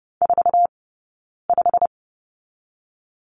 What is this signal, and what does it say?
Morse code, heard as a keyed tone, about 30 wpm, 700 Hz. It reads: 4 5
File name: Lesson 16d - Letter c - Two Characters 30wpm